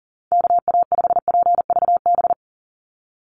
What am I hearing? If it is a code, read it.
Morse code, heard as a keyed tone, about 40 wpm, 700 Hz. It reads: KA5PVB
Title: NA SOTA Chaser Callsigns 40wpm